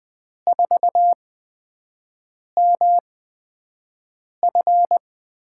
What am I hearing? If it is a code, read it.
Morse code, heard as a keyed tone, about 20 wpm, spaced as for 5 wpm, 700 Hz. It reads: 4MF
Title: USA Call Signs - 4 Characters - 8x Character Spacing - 20wpm